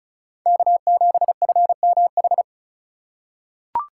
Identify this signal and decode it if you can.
Morse code, heard as a keyed tone, about 36 wpm, 700 Hz. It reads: K7FMH
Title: USA Call Signs - 3x Character Spacing - 35wpm